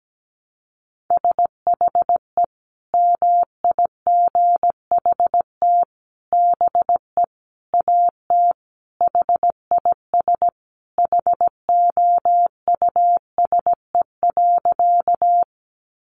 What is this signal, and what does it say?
Morse code, heard as a keyed tone, about 17 wpm, 700 Hz. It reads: SHE MIGHT BE AT HIS HOUSE.
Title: Sentences from Top 300 Words 17wpm